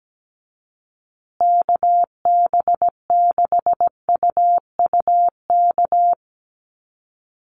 Morse code, heard as a keyed tone, about 17 wpm, 700 Hz. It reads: KB6UUK
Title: USA Call Signs Repeated 3x 17wpm